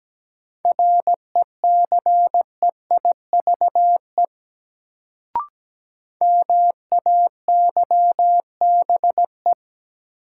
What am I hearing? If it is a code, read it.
Morse code, heard as a keyed tone, about 17 wpm, 700 Hz. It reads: RECEIVE MAYBE
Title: Sentences from Top 1000 Words - New Word Drill - 17wpm